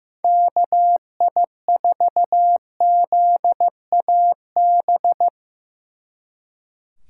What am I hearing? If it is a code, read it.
Morse code, heard as a keyed tone, about 17 wpm, 700 Hz. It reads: KI4ZAB
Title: Speed Racing - USA 2x3 Call Signs 15wpm